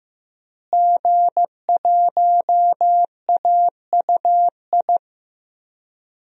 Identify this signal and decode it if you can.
Morse code, heard as a keyed tone, about 15 wpm, 700 Hz. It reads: G1AUI
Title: DXCC-100 Call Signs - 5 Characters - Repeated 2x - 15wpm